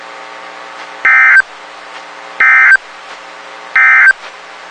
Emergency Alert System (SAME) tones, heard as harsh buzzing bursts. It is heard from 0.9 to 4.2 s.
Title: WXK91 NOAA Weather Radio